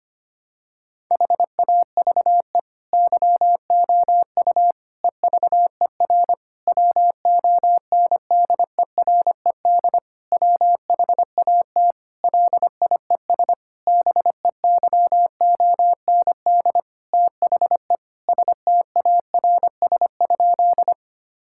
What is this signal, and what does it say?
Morse code, heard as a keyed tone, about 25 wpm, 700 Hz. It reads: HAVE YOU EVER WONDERED WHAT LIES BEYOND THE STARS?
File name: practice / 25wpm / Difficult Sentences with Hints - 25wpm